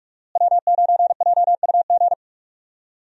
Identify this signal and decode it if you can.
Morse code, heard as a keyed tone, about 41 wpm, 700 Hz. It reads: W9JUG
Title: USA Call Signs - 5 Characters - Repeated 3x - 45wpm